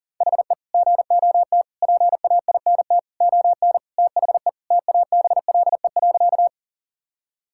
Morse code, heard as a keyed tone, about 38 wpm, 700 Hz. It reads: HE GOT PAINT ON THE TABLE.
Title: Sentences from Top 1000 Words - Review Part 6 - 40wpm